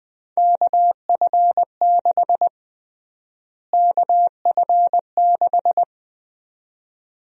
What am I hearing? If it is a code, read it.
Morse code, heard as a keyed tone, about 21 wpm, 700 Hz. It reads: KF6 KF6
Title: USA Call Sign Prefixes - Repeated 2x - 20wpm